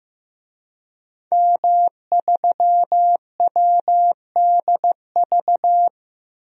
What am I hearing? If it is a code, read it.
Morse code, heard as a keyed tone, about 15 wpm, 700 Hz. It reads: M3WDV
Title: DXCC-100 Call Signs - 5 Characters - 15wpm